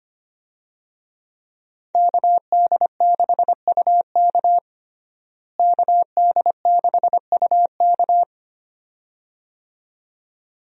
Morse code, heard as a keyed tone, about 25 wpm, 700 Hz. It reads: KD6UK KD6UK
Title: USA Call Signs - 5 Characters - Repeated 2x - 25wpm